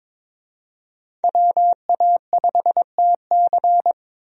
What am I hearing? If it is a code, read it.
Morse code, heard as a keyed tone, about 21 wpm, 700 Hz. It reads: WA5TC
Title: USA Call Signs - 5 Characters - Repeated 3x - 22wpm